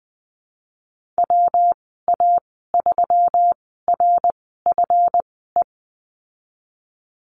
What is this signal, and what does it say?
Morse code, heard as a keyed tone, about 20 wpm, spaced as for 15 wpm, 700 Hz. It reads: WA3RFE
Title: 179 Lesson 29g - Slash - Call Signs 20wpm